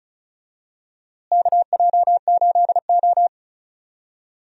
Morse code, heard as a keyed tone, about 36 wpm, 700 Hz. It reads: KJ8O
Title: USA Call Signs - 4 Characters - Repeated 2x - 35wpm